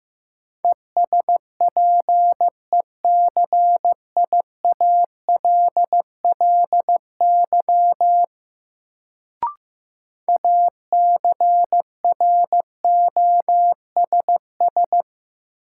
Morse code, heard as a keyed tone, about 15 wpm, 700 Hz. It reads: ESPECIALLY ACROSS
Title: Sentences from Top 700 Words - New Word Drill - 15wpm